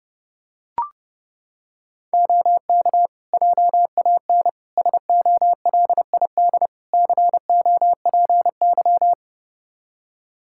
Morse code, heard as a keyed tone, about 32 wpm, 700 Hz. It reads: OK JAN SOLID COPY
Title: K1USN SST QSO Practice 30wpm